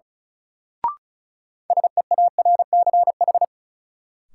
Morse code, heard as a keyed tone, about 36 wpm, 700 Hz. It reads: SEARCH
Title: Sentences from Top 800 Words - New Word Drill - 35wpm